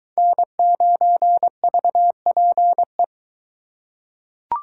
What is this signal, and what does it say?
Morse code, heard as a keyed tone, about 23 wpm, 700 Hz. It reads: N9VPE